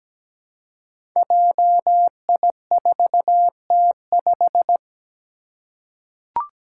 Morse code, heard as a keyed tone, about 17 wpm, 700 Hz. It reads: JI4T5
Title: Five Challenging Letter-Number Combinations 17wpm